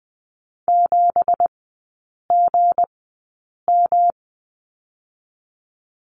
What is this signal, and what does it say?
Morse code, heard as a keyed tone, about 20 wpm, spaced as for 8 wpm, 700 Hz. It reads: 7GM
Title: 166 Lesson 27e - Number 7 - Three Characters 20wpm